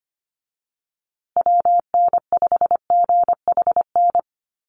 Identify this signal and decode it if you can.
Morse code, heard as a keyed tone, about 25 wpm, 700 Hz. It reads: WN5GHN
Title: USA Call Signs Repeated 2x 25wpm